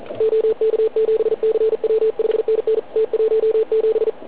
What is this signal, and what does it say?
Morse code, heard as a keyed tone, about 42 wpm, 440 Hz, with radio noise. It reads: OK7CW 5NN T17